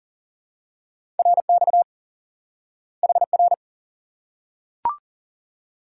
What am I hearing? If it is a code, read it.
Morse code, heard as a keyed tone, about 40 wpm, 700 Hz. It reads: RX HR
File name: IQE Course 3 - Lesson 89 - YAGI - Two Elements - 40wpm